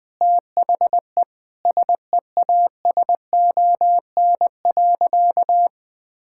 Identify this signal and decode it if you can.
Morse code, heard as a keyed tone, about 20 wpm, 700 Hz. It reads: THE SEASON.
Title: Sentences with 7-Word Phrases - 20wpm